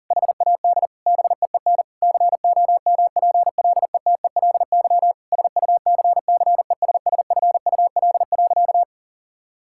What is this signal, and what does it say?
Morse code, heard as a keyed tone, about 40 wpm, 700 Hz. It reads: HAD BEEN COMPLETELY SUCCESSFUL.
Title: Sentences from Top 1600 Words 40wpm